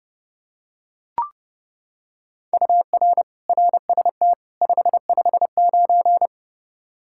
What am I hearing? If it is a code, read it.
Morse code, heard as a keyed tone, about 31 wpm, 700 Hz. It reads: UR RST 559
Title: Practice QSO - Exchange 1 - Beginning - 30wpm